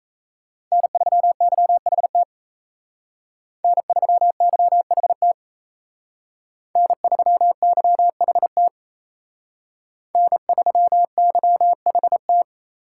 Morse code, heard as a keyed tone, about 36 wpm, 700 Hz. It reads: N3YHT N3YHT N3YHT N3YHT